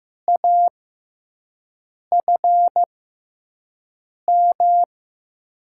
Morse code, heard as a keyed tone, about 15 wpm, 700 Hz. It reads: A F M
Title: USA Call Signs - 6x Character Spacing - 15wpm